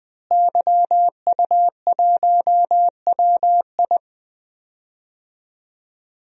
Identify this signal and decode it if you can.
Morse code, heard as a keyed tone, about 20 wpm, 700 Hz. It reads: YU1WI
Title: DXCC-100 Call Signs - 5 Characters - Repeated 2x - 20wpm